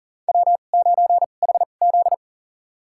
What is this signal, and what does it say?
Morse code, heard as a keyed tone, about 40 wpm, spaced as for 29 wpm, 690 Hz. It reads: W9HZ
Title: USA Call Signs - 4 Characters - 2x Character Spacing - 40wpm